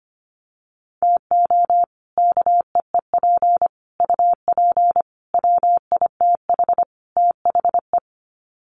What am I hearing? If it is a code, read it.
Morse code, heard as a keyed tone, about 25 wpm, 700 Hz. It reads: TO KEEP UP WITH THE
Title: Sets of 5 Words Encore 25wpm